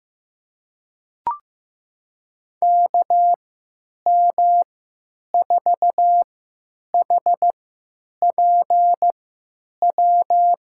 Morse code, heard as a keyed tone, about 15 wpm, spaced as for 9 wpm, 700 Hz. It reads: KM4HPW
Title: USA Call Signs - 6 Characters - 3x Character Spacing - 15wpm